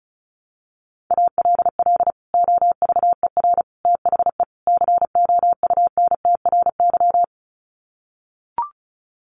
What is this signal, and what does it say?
Morse code, heard as a keyed tone, about 34 wpm, 700 Hz. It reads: ALL OVER THE COUNTRY
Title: Sets of 4 Words 35wpm